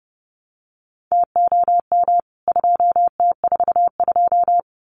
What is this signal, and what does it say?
Morse code, heard as a keyed tone, about 30 wpm, 700 Hz. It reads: TOM 2T42
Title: CWT Exchange 30wpm